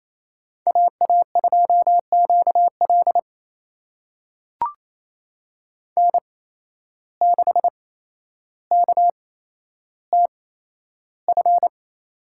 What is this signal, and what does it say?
Morse code, heard as a keyed tone, about 28 wpm, spaced as for 12 wpm, 700 Hz. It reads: AA2QL N6KTF
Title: USA Call Signs - 5 Characters - 8x Character Spacing - 28wpm